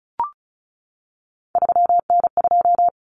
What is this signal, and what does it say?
Morse code, heard as a keyed tone, about 34 wpm, 700 Hz. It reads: 3N2
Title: Common Cut Numbers - 3-Digit 35wpm